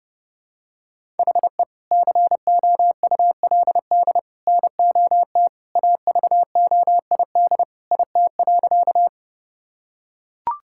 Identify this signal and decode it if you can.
Morse code, heard as a keyed tone, about 30 wpm, 700 Hz. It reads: HE COULD NOT AVOID IT.
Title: Sentences from Top 1000 Words - Review Part 6 - 30wpm